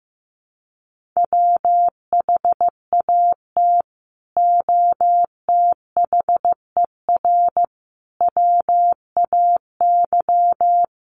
Morse code, heard as a keyed tone, about 15 wpm, 700 Hz. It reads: WHAT OTHER WAY
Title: Sentences from Top 100 Words 15wpm